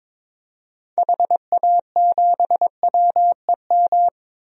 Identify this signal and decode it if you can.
Morse code, heard as a keyed tone, about 22 wpm, 700 Hz. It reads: HA7WEM
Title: DXCC-100 Call Signs - 6 Characters - 22wpm